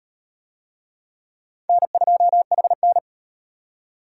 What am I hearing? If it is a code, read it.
Morse code, heard as a keyed tone, about 38 wpm, 700 Hz. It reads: N2HN